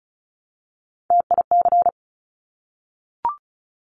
Morse code, heard as a keyed tone, about 36 wpm, 700 Hz. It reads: TIC
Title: Top Three Letter Combinations in Words 35wpm